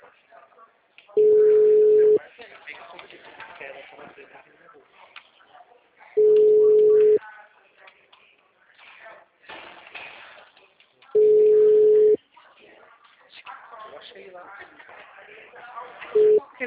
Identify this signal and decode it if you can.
Ringback tone (Europe, 425 Hz), with speech.